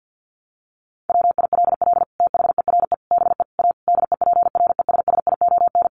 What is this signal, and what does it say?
Morse code, heard as a keyed tone, about 50 wpm, 700 Hz. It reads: WILL THERE BE A DEPRESSION
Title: Sentences from Top 1300 Words 50wpm